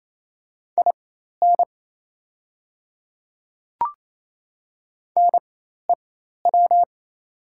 Morse code, heard as a keyed tone, about 28 wpm, spaced as for 13 wpm, 700 Hz. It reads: IN NEW
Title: 4x Character Spacing - Top 300 Words - 28wpm